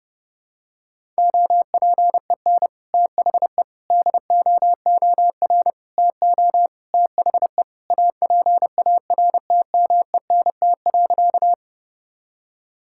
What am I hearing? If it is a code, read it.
Morse code, heard as a keyed tone, about 32 wpm, 700 Hz. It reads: OPEN THE DOOR TO THE APARTMENT.
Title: Sentences from Top 1000 Words - Review Part 6 - 30wpm